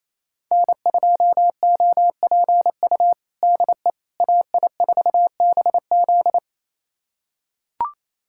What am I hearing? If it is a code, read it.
Morse code, heard as a keyed tone, about 28 wpm, 700 Hz. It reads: N2OPU DE AI4BZ
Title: USA Callsign DE Callsign 28wpm